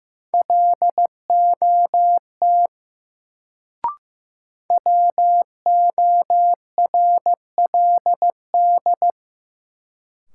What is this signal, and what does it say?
Morse code, heard as a keyed tone, about 15 wpm, 700 Hz. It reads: LOT WORLD
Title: Sets of 2 Words - New Word Drill - 15wpm